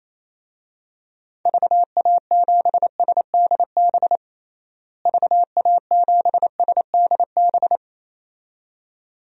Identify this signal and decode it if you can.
Morse code, heard as a keyed tone, about 28 wpm, 700 Hz. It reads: VA7SDB VA7SDB
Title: DXCC-100 Call Signs - 6 Characters - Repeated 2x - 28wpm